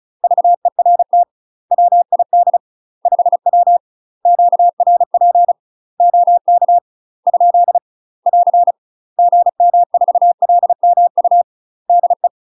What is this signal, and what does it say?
Morse code, heard as a keyed tone, about 36 wpm, 700 Hz. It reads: VERT WID 5W QRP OK ? <AR> GM4LMU DE
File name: UK MASTER QSOs TRAINING / 35WPM / UK MASTER QSOs 0011